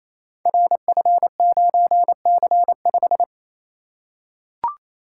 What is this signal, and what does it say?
Morse code, heard as a keyed tone, about 28 wpm, 700 Hz. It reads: RF9C5
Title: Code Groups - 5 Letter:Numbers - 28wpm